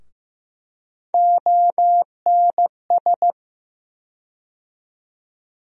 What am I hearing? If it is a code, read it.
Morse code, heard as a keyed tone, about 15 wpm, 700 Hz. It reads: ONS
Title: USA Call Sign Postfixes - 3 Letters 15wpm